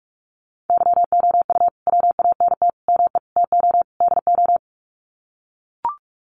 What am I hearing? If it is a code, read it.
Morse code, heard as a keyed tone, about 44 wpm, 700 Hz. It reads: YOU WANT ME TO DO
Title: Sets of 5 Words Encore2 45wpm